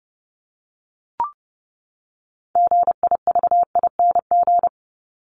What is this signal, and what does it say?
Morse code, heard as a keyed tone, about 30 wpm, 700 Hz. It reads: GIVING
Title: ING Suffix - Top 100 Words with Base Words 30wpm